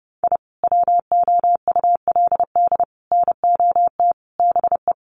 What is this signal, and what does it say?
Morse code, heard as a keyed tone, about 30 wpm, 700 Hz. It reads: I WOULD NOT BE
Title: Sets of 5 Words Encore2 30wpm